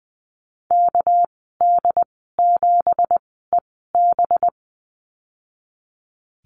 Morse code, heard as a keyed tone, about 20 wpm, spaced as for 14 wpm, 700 Hz. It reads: KD7EB
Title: 238 Lesson 38g - Letter q - Call Signs 20wpm